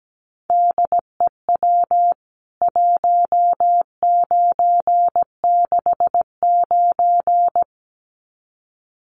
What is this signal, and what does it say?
Morse code, heard as a keyed tone, about 17 wpm, 700 Hz. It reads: DEW 1969
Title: CWT Exchange 17wpm